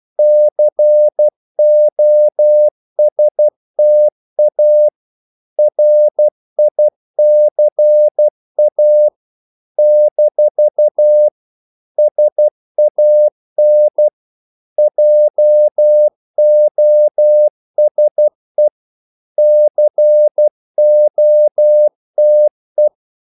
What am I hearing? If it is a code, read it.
Morse code, heard as a keyed tone, about 12 wpm, 600 Hz. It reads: COSTA RICA - SAN JOSE COTE
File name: CW / Practice / NationalCapitals12